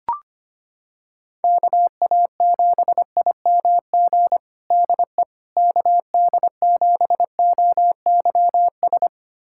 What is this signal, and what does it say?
Morse code, heard as a keyed tone, about 25 wpm, 700 Hz. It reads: KA7IMG DE KD7OYS
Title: USA Callsign DE Callsign - 6 Characters - 25wpm